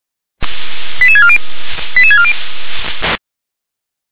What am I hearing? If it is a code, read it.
Five-tone selective call (ZVEI), heard as quick runs of five tones, tones of about 70 ms. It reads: 88640 88640